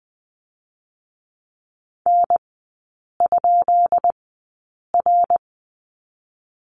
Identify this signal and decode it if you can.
Morse code, heard as a keyed tone, about 20 wpm, spaced as for 8 wpm, 700 Hz. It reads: N?R
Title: 145 Lesson 24e - Letter y - Three Characters 20wpm